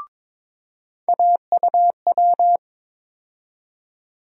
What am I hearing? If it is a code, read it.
Morse code, heard as a keyed tone, about 21 wpm, 700 Hz. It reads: AUW
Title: USA Call Sign Postfixes - 3 Letters 22wpm